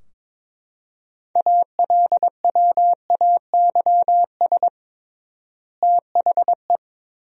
Morse code, heard as a keyed tone, about 22 wpm, 700 Hz. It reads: ALWAYS THE
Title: 3x Word-Spacing - Sentences from Top 300 Words - 22wpm